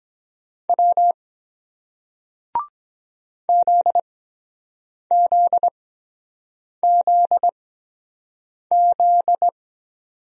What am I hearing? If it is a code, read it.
Morse code, heard as a keyed tone, about 23 wpm, spaced as for 7 wpm, 700 Hz. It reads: W ZZZZ